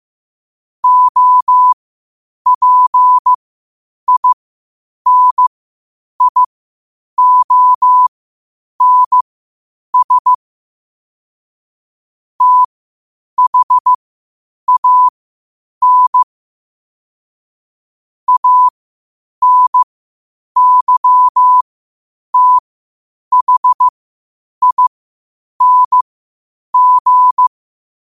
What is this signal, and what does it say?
Morse code, heard as a keyed tone, about 15 wpm, spaced as for 8 wpm, 1000 Hz. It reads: OPINIONS THAN ANYTHING
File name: qotdfiles / 8wpm